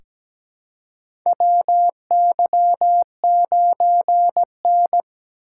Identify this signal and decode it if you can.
Morse code, heard as a keyed tone, about 17 wpm, 700 Hz. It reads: WY9N
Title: USA Call Signs - 4 Characters - 4x Character Spacing - 17wpm